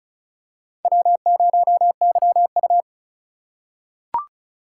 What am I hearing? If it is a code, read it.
Morse code, heard as a keyed tone, about 34 wpm, 700 Hz. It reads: W0YU
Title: USA Call Signs - 4 Characters - 4x Character Spacing - 35wpm